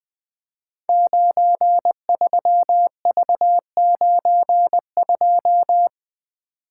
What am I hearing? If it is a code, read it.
Morse code, heard as a keyed tone, about 20 wpm, 700 Hz. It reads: 93V92
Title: Code Groups - 5 Letter:Numbers - 20wpm